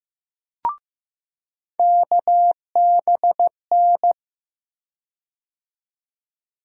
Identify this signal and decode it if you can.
Morse code, heard as a keyed tone, about 15 wpm, 700 Hz. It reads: KBN
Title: USA Call Sign Postfixes - 3 Letters 15wpm